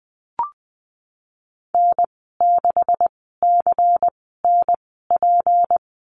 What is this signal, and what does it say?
Morse code, heard as a keyed tone, about 20 wpm, spaced as for 15 wpm, 700 Hz. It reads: N6CNP
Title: 154 Lesson 25g - Letter p - Call Signs 20wpm